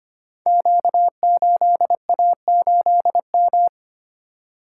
Morse code, heard as a keyed tone, about 25 wpm, 700 Hz. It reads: Q8A8M
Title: Code Groups - 5 Letter:Numbers - 25wpm